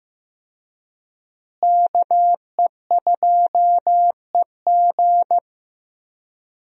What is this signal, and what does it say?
Morse code, heard as a keyed tone, about 15 wpm, 700 Hz. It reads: KE2EG
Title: USA Call Signs - 5 Characters - Repeated 2x - 15wpm